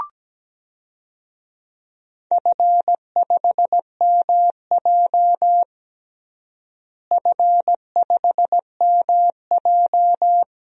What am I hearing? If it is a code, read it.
Morse code, heard as a keyed tone, about 17 wpm, 700 Hz. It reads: F5MJ F5MJ
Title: DXCC-100 Call Signs - 4 Characters - Repeated 2x - 17wpm